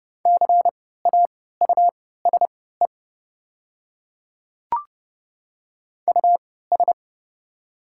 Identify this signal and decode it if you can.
Morse code, heard as a keyed tone, about 30 wpm, spaced as for 17 wpm, 700 Hz. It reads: CAUSE US
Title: 3x Character Spacing - Top 200 Words - 30wpm